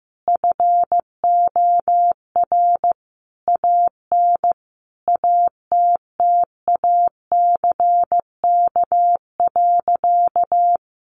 Morse code, heard as a keyed tone, about 15 wpm, 700 Hz. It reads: FOR AN ATTACK.
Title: Sentences from Top 1500 Words 15wpm